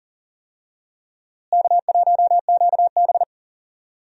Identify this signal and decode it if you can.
Morse code, heard as a keyed tone, about 37 wpm, 700 Hz. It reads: K1QB
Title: USA Call Signs - 4 Characters - Repeated 2x - 40wpm